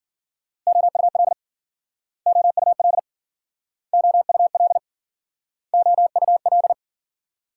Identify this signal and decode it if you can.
Morse code, heard as a keyed tone, about 52 wpm, 700 Hz. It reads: OUL OUL OUL OUL